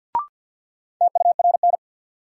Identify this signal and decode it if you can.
Morse code, heard as a keyed tone, about 53 wpm, 700 Hz. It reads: TURN
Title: Sentences from Top 200 Words - New Word Drill - 50wpm